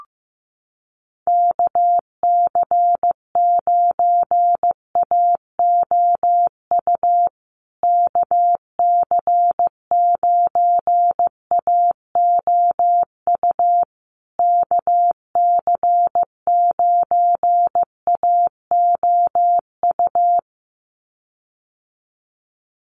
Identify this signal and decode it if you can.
Morse code, heard as a keyed tone, about 15 wpm, 700 Hz. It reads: KC9AOU KC9AOU KC9AOU
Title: USA Call Signs Repeated 3x 15wpm